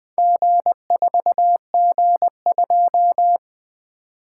Morse code, heard as a keyed tone, about 24 wpm, 700 Hz. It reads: G4G2
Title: Speed-Racing - Four Letter-Number Combinations - 20wpm